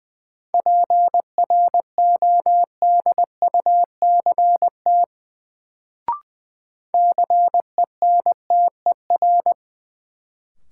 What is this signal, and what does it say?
Morse code, heard as a keyed tone, about 20 wpm, 700 Hz. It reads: PRODUCT CENTER
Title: Sentences from Top 1000 Words - Review Part 2 - New Word Drill - 20wpm